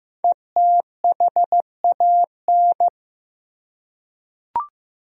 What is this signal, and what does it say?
Morse code, heard as a keyed tone, about 15 wpm, 700 Hz. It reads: ETHAN
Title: USA Common Names - 15wpm